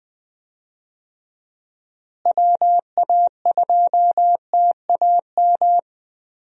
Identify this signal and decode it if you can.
Morse code, heard as a keyed tone, about 20 wpm, 700 Hz. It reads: WA2TAM
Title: USA Call Signs - 6 Characters - Repeated 3x - 20wpm